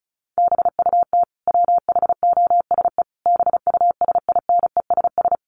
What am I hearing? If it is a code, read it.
Morse code, heard as a keyed tone, about 34 wpm, 700 Hz. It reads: BUT WHOSE BUSINESS
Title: Sentences from Top 1500 Words 35wpm